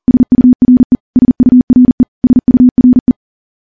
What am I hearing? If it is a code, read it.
Morse code, heard as a keyed tone, about 40 wpm, 250 Hz. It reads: SURE SURE SURE
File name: morse-training-40WPM